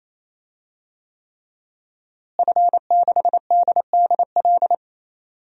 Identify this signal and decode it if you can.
Morse code, heard as a keyed tone, about 28 wpm, 700 Hz. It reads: F6DDL
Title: DXCC-100 Call Signs - 5 Characters - Repeated 3x - 28wpm